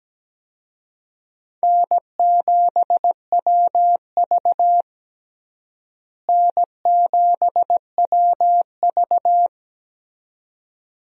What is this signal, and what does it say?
Morse code, heard as a keyed tone, about 17 wpm, 700 Hz. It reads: N7WV N7WV
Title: USA Call Signs - 4 Characters - Repeated 2x - 17wpm